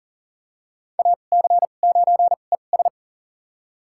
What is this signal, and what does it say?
Morse code, heard as a keyed tone, about 40 wpm, spaced as for 29 wpm, 700 Hz. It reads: AC9ES
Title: USA Call Signs - 2x Character Spacing - 40wpm